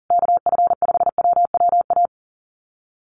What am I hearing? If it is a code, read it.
Morse code, heard as a keyed tone, about 40 wpm, 700 Hz. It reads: KF5WWA
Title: USA Call Signs Repeated 3x 40wpm